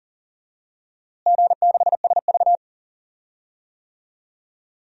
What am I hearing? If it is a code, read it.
Morse code, heard as a keyed tone, about 40 wpm, 700 Hz. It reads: G6SV
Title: DXCC-100 Call Signs - 4 Characters - Repeated 2x - 40wpm